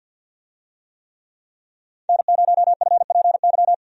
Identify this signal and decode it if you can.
Morse code, heard as a keyed tone, about 47 wpm, 700 Hz. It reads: N0FPY
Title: USA Call Signs - 5 Characters - 50wpm